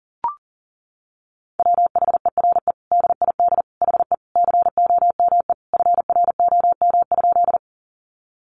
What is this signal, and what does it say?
Morse code, heard as a keyed tone, about 40 wpm, 700 Hz. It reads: WHERE DID HE COME FROM?
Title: Sentences from Top 200 Words 40wpm